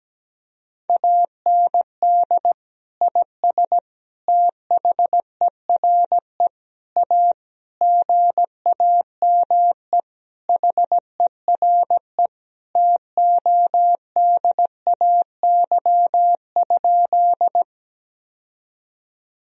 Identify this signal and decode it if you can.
Morse code, heard as a keyed tone, about 17 wpm, 700 Hz. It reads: AND IS THERE A GAME HERE TODAY?
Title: Sentences from Top 1000 Words - Review Part 3 - 17wpm